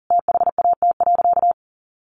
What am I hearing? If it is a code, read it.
Morse code, heard as a keyed tone, about 40 wpm, 700 Hz. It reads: THAT.
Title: Sentences from Top 400 Words 40wpm